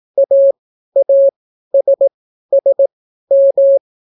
Morse code, heard as a keyed tone, about 18 wpm, spaced as for 12 wpm, 550 Hz. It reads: AASSM